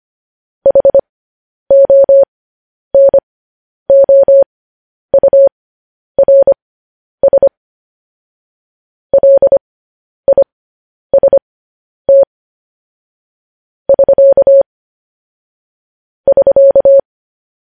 Morse code, heard as a keyed tone, about 25 wpm, spaced as for 10 wpm, 550 Hz. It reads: HONOURS LIST <SK> <SK>
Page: news headlines in morse code at 10 wpm